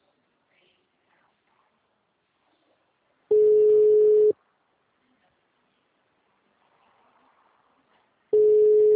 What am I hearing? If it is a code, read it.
Ringback tone (Europe, 425 Hz).